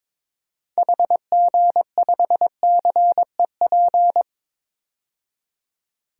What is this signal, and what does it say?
Morse code, heard as a keyed tone, about 21 wpm, 700 Hz. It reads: HG5CEP
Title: DXCC-100 Call Signs - 6 Characters - 22wpm